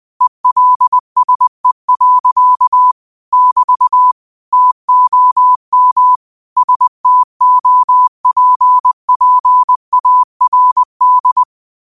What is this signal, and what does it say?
Morse code, heard as a keyed tone, about 20 wpm, 1000 Hz. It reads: ELSE. <BT> TOM STOPPARD